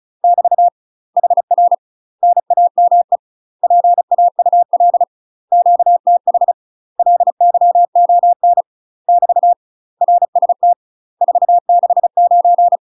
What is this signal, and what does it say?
Morse code, heard as a keyed tone, about 34 wpm, 700 Hz. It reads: <BT> HR NAME PAUL QTH LYON <BT> RST 469
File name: UK MASTER QSOs TRAINING / 35WPM / UK MASTER QSOs 0001